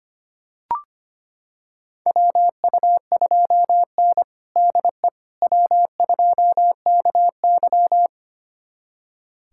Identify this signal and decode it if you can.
Morse code, heard as a keyed tone, about 25 wpm, 700 Hz. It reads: WU2N DE W2KY
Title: USA Callsign DE Callsign - 4 Characters - 25wpm